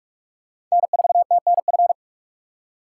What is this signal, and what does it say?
Morse code, heard as a keyed tone, about 44 wpm, 690 Hz. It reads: N4TNF
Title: USA Call Signs - 5 Characters - 6x Character Spacing - 45wpm